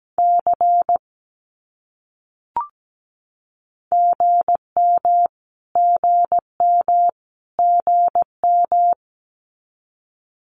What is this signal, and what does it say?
Morse code, heard as a keyed tone, about 17 wpm, 700 Hz. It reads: C GM GM GM
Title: Abbreviations Repeated 3x - 17wpm